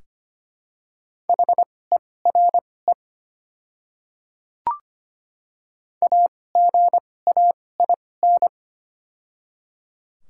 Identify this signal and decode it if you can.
Morse code, heard as a keyed tone, about 25 wpm, spaced as for 18 wpm, 700 Hz. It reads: HERE AGAIN
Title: 2x Character Spacing - Top 200 Words - 25wpm